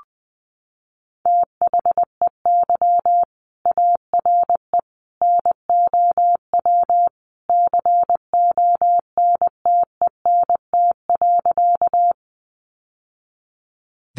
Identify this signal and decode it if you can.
Morse code, heard as a keyed tone, about 20 wpm, 700 Hz. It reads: THEY ARE NOW CONTENT.
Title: Sentences from Top 900 Words 20wpm